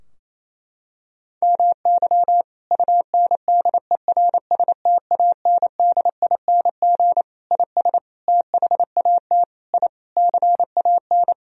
Morse code, heard as a keyed tone, about 28 wpm, 700 Hz. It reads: MY UNDERSTANDING IS THAT I CAN
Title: Sentences from Top 1000 Words - Review Part 3 - 28wpm